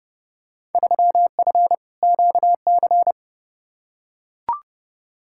Morse code, heard as a keyed tone, about 30 wpm, 700 Hz. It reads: 3F QC
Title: Field Day Exchanges - 30wpm